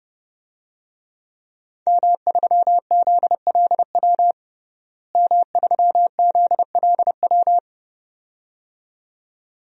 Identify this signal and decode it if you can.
Morse code, heard as a keyed tone, about 30 wpm, 700 Hz. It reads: M3ZLW M3ZLW
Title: DXCC-100 Call Signs - 5 Characters - Repeated 2x - 30wpm